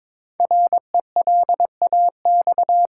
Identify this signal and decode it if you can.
Morse code, heard as a keyed tone, about 22 wpm, 700 Hz. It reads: RELAX
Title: Sentences from Top 1400 Words - New Word Drill - 22wpm